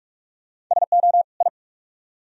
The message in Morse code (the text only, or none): SO I